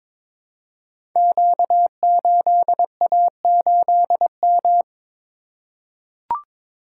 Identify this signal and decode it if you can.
Morse code, heard as a keyed tone, about 23 wpm, 690 Hz. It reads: Q8A8M
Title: Code Groups - 5 Letter:Numbers - 22wpm